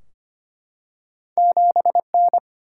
Morse code, heard as a keyed tone, about 25 wpm, 700 Hz. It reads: 7N
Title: Single-Letter Single-Number 25wpm